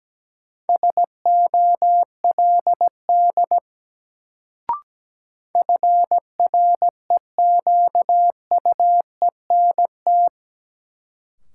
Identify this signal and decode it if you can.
Morse code, heard as a keyed tone, about 17 wpm, 700 Hz. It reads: SOLD FREQUENT
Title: Sentences from Top 1400 Words - New Word Drill - 17wpm